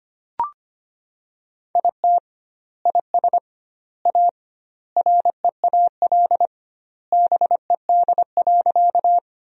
Morse code, heard as a keyed tone, about 25 wpm, 700 Hz. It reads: IT IS A REAL BED.
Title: 2x Word-Spacing - Sentences from Top 500 Words - 25wpm